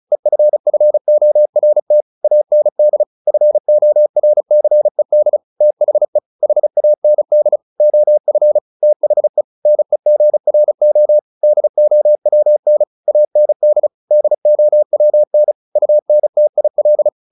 Morse code, heard as a keyed tone, about 34 wpm, 600 Hz. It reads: EFFORT AND FORCED THE HAND OF THE NEGRO DOWN AND DOWN UNTIL
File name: the old man and the sea 2 35wpm